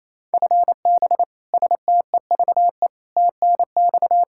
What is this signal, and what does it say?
Morse code, heard as a keyed tone, about 28 wpm, 700 Hz. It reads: FB STEVE TNX
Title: Practice QSO - Exchange 3 - More Info - 28wpm